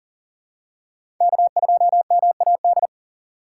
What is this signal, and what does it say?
Morse code, heard as a keyed tone, about 40 wpm, 690 Hz. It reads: K2MAD
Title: USA Call Signs - 3x Character Spacing - 40wpm